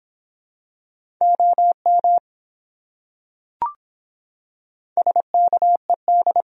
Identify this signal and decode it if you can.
Morse code, heard as a keyed tone, about 26 wpm, 700 Hz. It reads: OM SKED